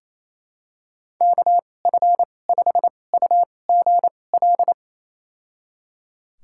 Morse code, heard as a keyed tone, about 28 wpm, spaced as for 20 wpm, 700 Hz. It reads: KF5UGL
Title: USA Call Signs - 2x Character Spacing - 28wpm